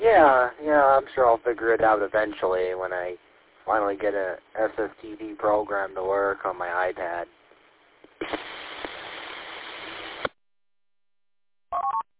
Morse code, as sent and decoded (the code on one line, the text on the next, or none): none